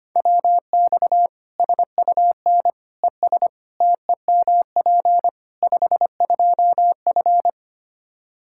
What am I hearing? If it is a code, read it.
Morse code, heard as a keyed tone, about 25 wpm, 700 Hz. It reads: WX SUN ES TEMP 52F
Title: Practice QSO - Exchange 2 - Info - 25wpm